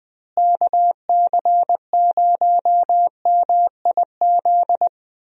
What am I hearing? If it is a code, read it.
Morse code, heard as a keyed tone, about 20 wpm, 700 Hz. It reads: KC0MIZ
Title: USA Call Signs - 6 Characters - Repeated 2x - 20wpm